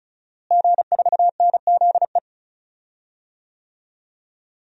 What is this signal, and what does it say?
Morse code, heard as a keyed tone, about 36 wpm, 700 Hz. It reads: G4NZE
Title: DXCC-100 Call Signs - 5 Characters - Repeated 2x - 35wpm